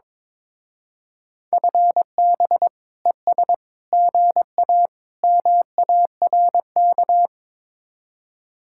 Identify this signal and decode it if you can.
Morse code, heard as a keyed tone, about 22 wpm, 700 Hz. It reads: FB ES GA MARK
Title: Speed-Racing Common QSO Phrases - 22wpm